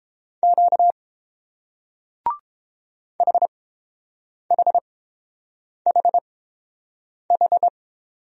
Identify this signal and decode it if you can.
Morse code, heard as a keyed tone, about 29 wpm, spaced as for 7 wpm, 700 Hz. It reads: Q HHHH